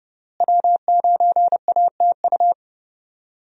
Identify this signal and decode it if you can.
Morse code, heard as a keyed tone, about 30 wpm, 700 Hz. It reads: W9ATU
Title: USA Call Signs - 5 Characters - Repeated 3x - 30wpm